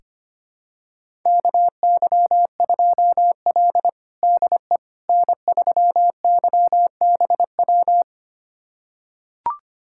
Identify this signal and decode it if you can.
Morse code, heard as a keyed tone, about 25 wpm, 700 Hz. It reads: KY2L DE N3YBW
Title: USA Callsign DE Callsign 25wpm